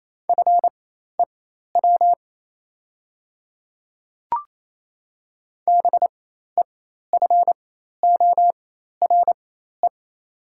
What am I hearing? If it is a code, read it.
Morse code, heard as a keyed tone, about 28 wpm, spaced as for 13 wpm, 700 Hz. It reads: FEW BEFORE
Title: 4x Character Spacing - Top 300 Words - 28wpm